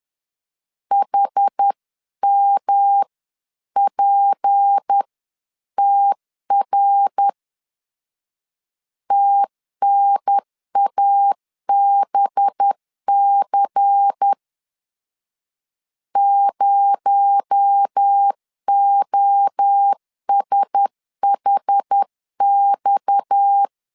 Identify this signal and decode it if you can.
Morse code, heard as a keyed tone, about 10 wpm, 800 Hz. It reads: HMPTR TNABC 0OSHX
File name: CW / DET / 1 6 1500C S C 10wpm 4252